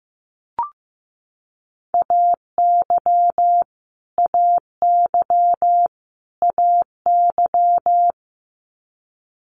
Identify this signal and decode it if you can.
Morse code, heard as a keyed tone, about 15 wpm, 700 Hz. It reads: AY AY AY
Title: Multi-letter Phonemes Repeated 3x 15wpm